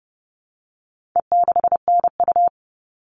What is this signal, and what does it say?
Morse code, heard as a keyed tone, about 30 wpm, 700 Hz. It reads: E6NU
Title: All Cut Numbers - 4-Digit 30wpm